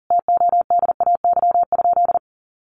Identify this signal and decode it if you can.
Morse code, heard as a keyed tone, about 40 wpm, 700 Hz. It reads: TODAY?
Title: Sentences from Top 1200 Words 40wpm